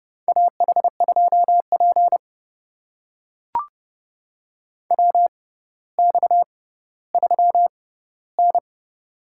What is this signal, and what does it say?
Morse code, heard as a keyed tone, about 30 wpm, spaced as for 15 wpm, 700 Hz. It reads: AH2P WX3N
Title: USA Call Signs - 6x Character Spacing - 30wpm